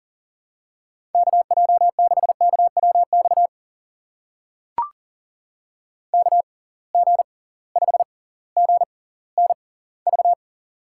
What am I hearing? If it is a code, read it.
Morse code, heard as a keyed tone, about 40 wpm, spaced as for 20 wpm, 700 Hz. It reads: KJ6KWX KG5GNV
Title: USA Call Signs - 6 Characters - 6x Character Spacing - 40wpm